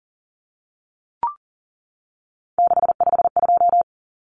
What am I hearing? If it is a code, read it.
Morse code, heard as a keyed tone, about 39 wpm, 700 Hz. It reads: 652
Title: Numbers - 3-Digit 40wpm